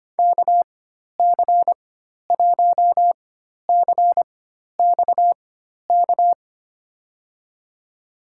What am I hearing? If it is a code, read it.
Morse code, heard as a keyed tone, about 25 wpm, spaced as for 12 wpm, 700 Hz. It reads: KC1CXK
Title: USA Call Signs - 6 Characters - 4x Character Spacing - 25wpm